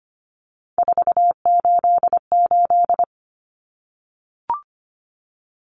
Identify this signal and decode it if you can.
Morse code, heard as a keyed tone, about 25 wpm, 700 Hz. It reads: 488
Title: Numbers - 3-Digit 25wpm